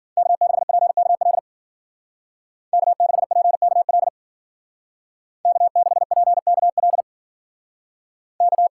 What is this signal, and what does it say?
Morse code, heard as a keyed tone, about 53 wpm, 700 Hz. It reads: K6PKL K6PKL K6PKL K